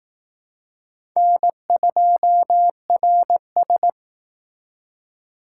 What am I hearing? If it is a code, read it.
Morse code, heard as a keyed tone, about 19 wpm, 700 Hz. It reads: N2RS
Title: Speed Racing - USA 1x2 Call Signs 15wpm